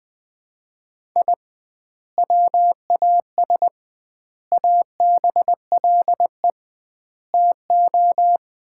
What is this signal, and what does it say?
Morse code, heard as a keyed tone, about 21 wpm, 700 Hz. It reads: I WAS ABLE TO
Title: 2x Word-Spacing - Sets of 4 Words - 20wpm